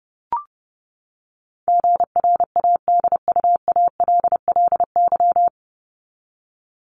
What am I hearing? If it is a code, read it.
Morse code, heard as a keyed tone, about 30 wpm, 700 Hz. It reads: GRADUALLY
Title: LY Suffix - Top 100 Words with Base Words 30wpm